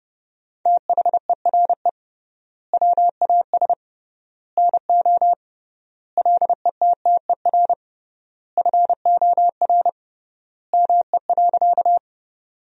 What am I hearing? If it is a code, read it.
Morse code, heard as a keyed tone, about 32 wpm, 700 Hz. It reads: THERE WAS NO LETTER FOR ME.
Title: 3x Word-Spacing - Sentences from Top 300 Words - 30wpm